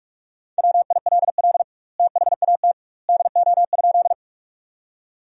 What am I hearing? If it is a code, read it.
Morse code, heard as a keyed tone, about 44 wpm, 700 Hz. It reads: WILL THAT DO?
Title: 2x Word-Spacing - Sentences from Top 100 Words - 45wpm